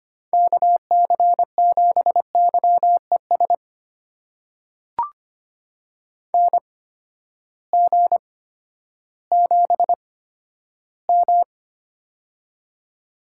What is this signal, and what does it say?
Morse code, heard as a keyed tone, about 25 wpm, spaced as for 12 wpm, 700 Hz. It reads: KC7YES NG7M
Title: USA Call Signs - 8x Character Spacing - 25wpm